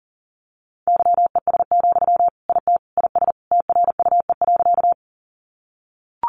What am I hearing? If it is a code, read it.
Morse code, heard as a keyed tone, about 40 wpm, 700 Hz. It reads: YES, IT IS TRUE.
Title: Sentences from Top 500 Words 40wpm